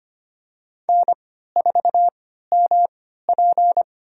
Morse code, heard as a keyed tone, about 25 wpm, spaced as for 14 wpm, 700 Hz. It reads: N4MP
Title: USA Call Signs - 4 Characters - 3x Character Spacing - 25wpm